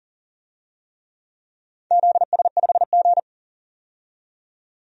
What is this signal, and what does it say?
Morse code, heard as a keyed tone, about 40 wpm, 700 Hz. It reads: ZS5G
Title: DXCC-100 Call Signs - 4 Characters - 40wpm